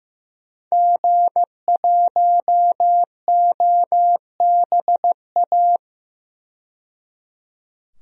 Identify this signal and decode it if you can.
Morse code, heard as a keyed tone, about 15 wpm, 700 Hz. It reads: G1OBA
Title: DXCC-100 Call Signs - 5 Characters - 15wpm